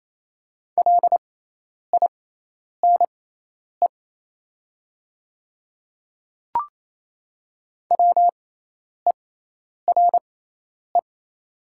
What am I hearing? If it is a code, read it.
Morse code, heard as a keyed tone, about 28 wpm, spaced as for 10 wpm, 700 Hz. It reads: LINE WERE
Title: 6x Character Spacing - Top 200 Words - 28wpm